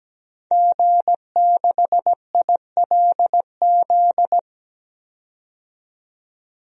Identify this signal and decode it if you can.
Morse code, heard as a keyed tone, about 17 wpm, 700 Hz. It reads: G6ILZ
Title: DXCC-100 Call Signs - 5 Characters - 17wpm